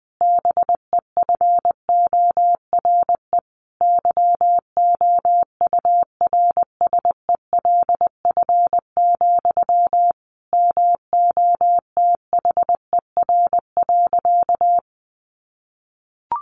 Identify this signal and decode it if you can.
Morse code, heard as a keyed tone, about 20 wpm, 700 Hz. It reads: BEFORE YOURSELF, MOTHER.
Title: Sentences from Top 1300 Words 20wpm